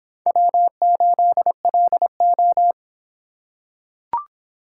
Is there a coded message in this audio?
Yes: Morse code from 0.3 to 2.7 s, heard as a keyed tone, about 24 wpm, 700 Hz. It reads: W8LO